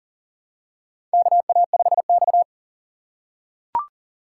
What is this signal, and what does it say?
Morse code, heard as a keyed tone, about 39 wpm, 690 Hz. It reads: KA5X
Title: USA Call Signs - 4 Characters - 2x Character Spacing - 40wpm